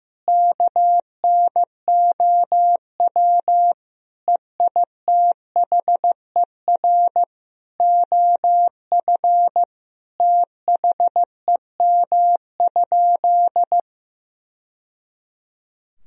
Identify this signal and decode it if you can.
Morse code, heard as a keyed tone, about 15 wpm, 700 Hz. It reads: KNOW EITHER OF THEM?
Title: Sentences from Top 1000 Words - Review Part 4 - 15wpm